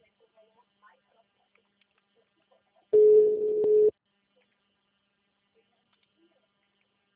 Ringback tone (Europe, 425 Hz).
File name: out- -2002- - - .367